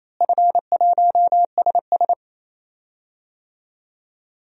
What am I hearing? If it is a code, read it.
Morse code, heard as a keyed tone, about 28 wpm, 700 Hz. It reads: F1SS
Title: DXCC-100 Call Signs - 4 Characters - Repeated 2x - 28wpm